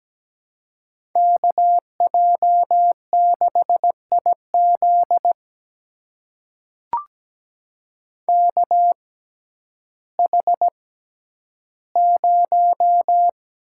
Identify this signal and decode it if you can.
Morse code, heard as a keyed tone, about 17 wpm, spaced as for 10 wpm, 700 Hz. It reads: KJ6IZ KH0
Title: USA Call Signs - 5 Characters - 6x Character Spacing - 17wpm